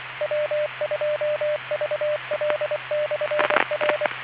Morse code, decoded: W2VL/R